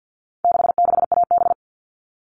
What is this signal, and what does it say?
Morse code, heard as a keyed tone, about 50 wpm, 700 Hz. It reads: 66AB
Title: All Cut Numbers - 4-Digit 50wpm